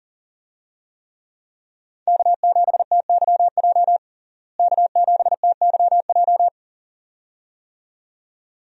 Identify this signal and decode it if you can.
Morse code, heard as a keyed tone, about 37 wpm, 700 Hz. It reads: K7TYJ K7TYJ
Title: USA Call Signs - 5 Characters - Repeated 2x - 40wpm